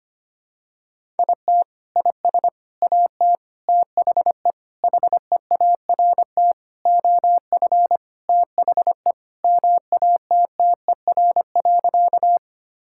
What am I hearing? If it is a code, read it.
Morse code, heard as a keyed tone, about 25 wpm, 700 Hz. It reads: IT IS AT THE HEART OF THE MATTER.
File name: practice / 25wpm / Sentences with 4-Word Phrases - 25wpm